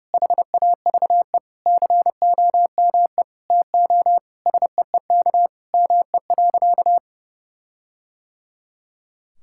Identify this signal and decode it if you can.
Morse code, heard as a keyed tone, about 30 wpm, 700 Hz. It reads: HAVE COME TO SEEK ME.
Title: Sentences from Top 1600 Words 30wpm